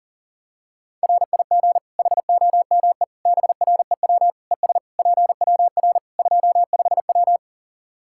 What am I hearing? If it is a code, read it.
Morse code, heard as a keyed tone, about 37 wpm, 700 Hz. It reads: RIG HOME BREW ES PWR 25W
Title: Practice QSO - Exchange 2 - Info - 40wpm